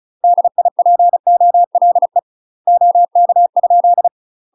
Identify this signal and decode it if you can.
Morse code, heard as a keyed tone, about 36 wpm, 700 Hz. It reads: DIPOLE OK ?
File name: UK MASTER QSOs TRAINING / 35WPM / UK MASTER QSOs 0027